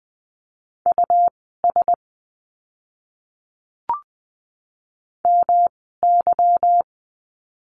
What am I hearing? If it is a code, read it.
Morse code, heard as a keyed tone, about 20 wpm, spaced as for 15 wpm, 700 Hz. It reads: US MY
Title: 146 Lesson 24f - Letter y - Words 20wpm